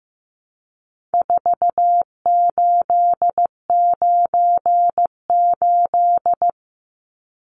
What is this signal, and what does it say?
Morse code, heard as a keyed tone, about 15 wpm, 700 Hz. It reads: 4898
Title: Numbers - 4-Digit 15wpm